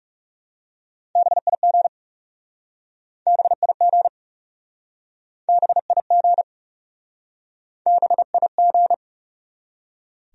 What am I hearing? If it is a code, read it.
Morse code, heard as a keyed tone, about 40 wpm, 700 Hz. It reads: BIG BIG BIG BIG